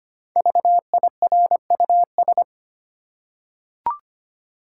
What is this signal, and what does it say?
Morse code, heard as a keyed tone, about 25 wpm, 700 Hz. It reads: VIRUS
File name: 5 Letter Words - First 3 Letters Given - 25wpm